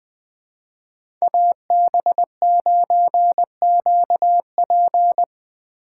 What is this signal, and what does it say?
Morse code, heard as a keyed tone, about 20 wpm, 700 Hz. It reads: AB9QP
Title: USA Call Signs - 5 Characters - 4x Character Spacing - 20wpm